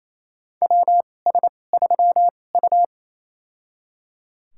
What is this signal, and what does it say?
Morse code, heard as a keyed tone, about 28 wpm, spaced as for 20 wpm, 700 Hz. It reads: WS3U
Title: USA Call Signs - 4 Characters - 2x Character Spacing - 28wpm